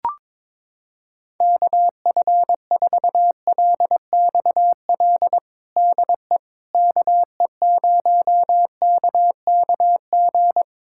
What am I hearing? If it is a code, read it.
Morse code, heard as a keyed tone, about 22 wpm, 700 Hz. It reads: KF4LXL DE KE0KKG
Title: USA Callsign DE Callsign - 6 Characters - 22wpm